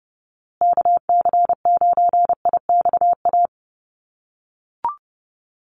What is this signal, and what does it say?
Morse code, heard as a keyed tone, about 30 wpm, 700 Hz. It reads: KC9IXA
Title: USA Call Signs Repeated 3x 30wpm